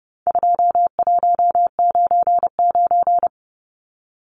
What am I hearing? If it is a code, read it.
Morse code, heard as a keyed tone, about 30 wpm, 700 Hz. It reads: 2199
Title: Numbers - 4-Digit 30wpm